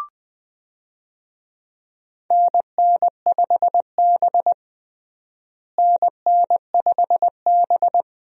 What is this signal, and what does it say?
Morse code, heard as a keyed tone, about 20 wpm, 700 Hz. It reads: NN5B NN5B
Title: USA Call Signs - 4 Characters - Repeated 2x - 20wpm